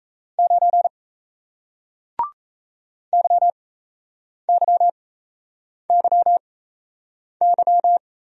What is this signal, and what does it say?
Morse code, heard as a keyed tone, about 38 wpm, spaced as for 8 wpm, 700 Hz. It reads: 9 YYYY